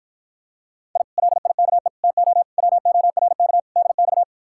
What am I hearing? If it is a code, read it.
Morse code, heard as a keyed tone, about 54 wpm, 700 Hz. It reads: I LIKE TO WORK DX